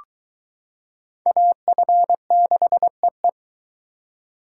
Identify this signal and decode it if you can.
Morse code, heard as a keyed tone, about 22 wpm, 700 Hz. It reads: AF6EE